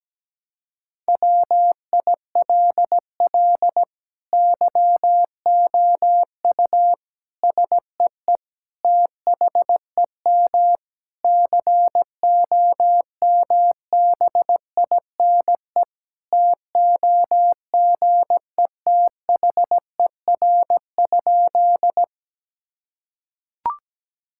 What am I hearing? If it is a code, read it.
Morse code, heard as a keyed tone, about 17 wpm, 700 Hz. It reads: WILL YOU SEE THEM COMBINE TOGETHER?
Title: Sentences from Top 1600 Words 17wpm